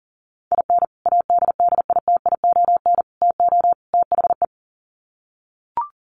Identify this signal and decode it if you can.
Morse code, heard as a keyed tone, about 40 wpm, 700 Hz. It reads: IN ADDITION TO THE
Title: Sets of 4 Words 40wpm